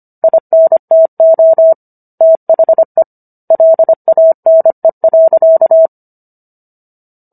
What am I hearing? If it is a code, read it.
Morse code, heard as a keyed tone, about 25 wpm, 650 Hz. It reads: INTO THE LANE.